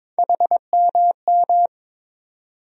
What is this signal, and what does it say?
Morse code, heard as a keyed tone, about 22 wpm, 700 Hz. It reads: HMM
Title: USA Call Sign Postfixes - 3 Letters 22wpm